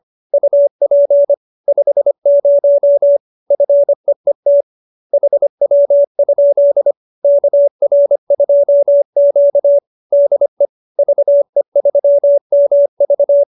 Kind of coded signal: Morse code

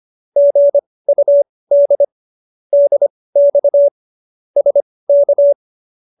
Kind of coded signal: Morse code

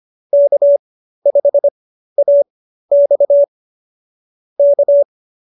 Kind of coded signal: Morse code

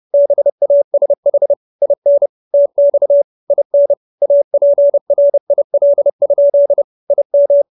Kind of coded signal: Morse code

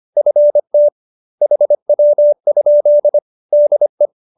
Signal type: Morse code

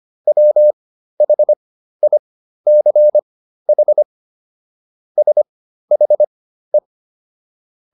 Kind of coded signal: Morse code

